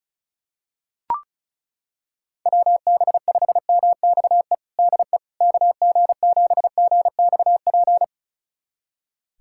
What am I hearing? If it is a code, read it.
Morse code, heard as a keyed tone, about 36 wpm, 700 Hz. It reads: WB5MXE DE KG7GXP
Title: USA Callsign DE Callsign - 6 Characters - 35wpm